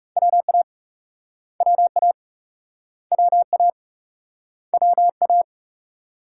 Morse code, heard as a keyed tone, about 38 wpm, 700 Hz. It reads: WA WA WA WA